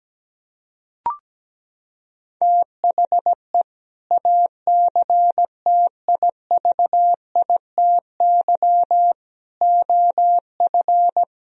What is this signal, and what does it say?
Morse code, heard as a keyed tone, about 17 wpm, 700 Hz. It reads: THE ACTIVITY OF
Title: Sentences from Top 1000 Words - Review Part 4 - 17wpm